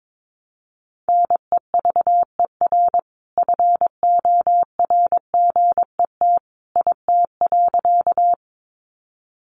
Morse code, heard as a keyed tone, about 22 wpm, 700 Hz. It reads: NEVER FORGET IT.
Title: Sentences from Top 1200 Words 22wpm